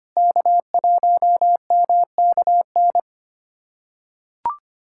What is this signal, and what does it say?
Morse code, heard as a keyed tone, about 25 wpm, 700 Hz. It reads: K1MKN
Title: USA Callsign DE Callsign 25wpm